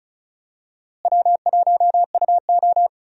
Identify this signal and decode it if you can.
Morse code, heard as a keyed tone, about 34 wpm, 700 Hz. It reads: W1UO
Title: USA Call Signs - 4 Characters - Repeated 2x - 35wpm